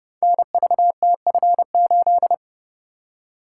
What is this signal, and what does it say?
Morse code, heard as a keyed tone, about 30 wpm, 700 Hz. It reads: NVTF8
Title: Code Groups - 5 Letter:Numbers - 30wpm